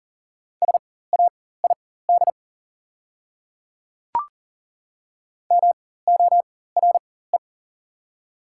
Morse code, heard as a keyed tone, about 40 wpm, spaced as for 19 wpm, 690 Hz. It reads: SAID MORE
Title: 4x Character Spacing - Top 200 Words - 40wpm